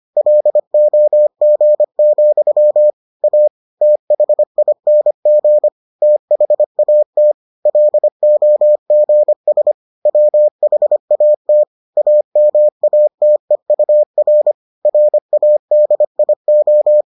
Morse code, heard as a keyed tone, about 25 wpm, 600 Hz. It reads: LOG, A THING THAT LOGS WHAT AMATEUR RADIO